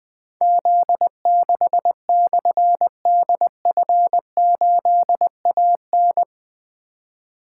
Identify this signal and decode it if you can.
Morse code, heard as a keyed tone, about 20 wpm, 700 Hz. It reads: Z6/DF8AN
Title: Difficult Call Signs 20wpm